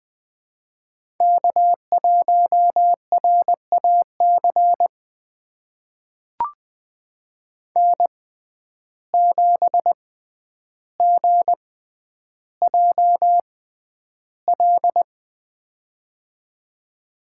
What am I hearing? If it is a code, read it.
Morse code, heard as a keyed tone, about 20 wpm, spaced as for 10 wpm, 700 Hz. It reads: K1RAC N7GJL
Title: USA Call Signs - 5 Characters - 6x Character Spacing - 20wpm